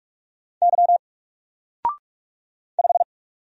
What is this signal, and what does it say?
Morse code, heard as a keyed tone, about 46 wpm, 700 Hz. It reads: Y 5
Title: Single Character (Letter:Number:Punctuation:Prosign) 45wpm